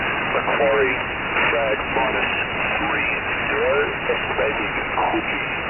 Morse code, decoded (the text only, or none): none